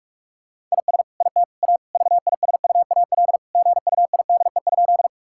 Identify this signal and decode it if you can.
Morse code, heard as a keyed tone, about 46 wpm, 700 Hz. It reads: IS IT A VISUAL GUIDE?
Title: Sentences from Top 1000 Words - Review Part 6 - 45wpm